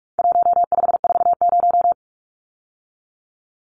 Morse code, heard as a keyed tone, about 44 wpm, 700 Hz. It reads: 1540
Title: Numbers - 4-Digit 45wpm